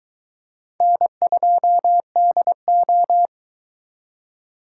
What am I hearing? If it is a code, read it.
Morse code, heard as a keyed tone, about 22 wpm, 700 Hz. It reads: N2DO